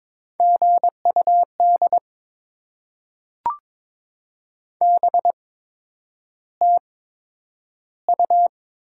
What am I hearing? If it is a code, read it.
Morse code, heard as a keyed tone, about 22 wpm, spaced as for 9 wpm, 700 Hz. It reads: GUD BTU
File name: Top 20 QSO Elements - 8x Character Spacing - 22wpm